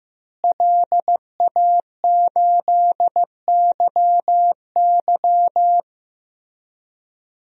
Morse code, heard as a keyed tone, about 15 wpm, 700 Hz. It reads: LA8YY
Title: DXCC-100 Call Signs - 5 Characters - Repeated 3x - 15wpm